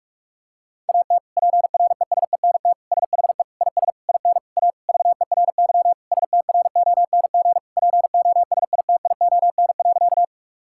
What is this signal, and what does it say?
Morse code, heard as a keyed tone, about 41 wpm, 700 Hz. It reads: AT PRESENT SHE IS IN A VERY STRONG POSITION.
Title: Sentences from Top 1000 Words - Review Part 5 - 45wpm